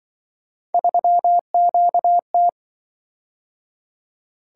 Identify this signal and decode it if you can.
Morse code, heard as a keyed tone, about 25 wpm, 700 Hz. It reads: 3QT